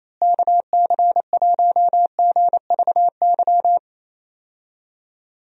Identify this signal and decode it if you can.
Morse code, heard as a keyed tone, about 28 wpm, 700 Hz. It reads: KC1GVY
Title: USA Call Signs - 6 Characters - Repeated 2x - 28wpm